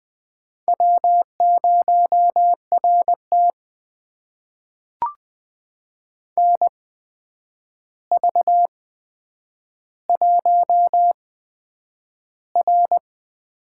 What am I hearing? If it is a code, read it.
Morse code, heard as a keyed tone, about 20 wpm, spaced as for 9 wpm, 700 Hz. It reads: W0RT NV1R
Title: USA Call Signs - 4 Characters - 8x Character Spacing - 20wpm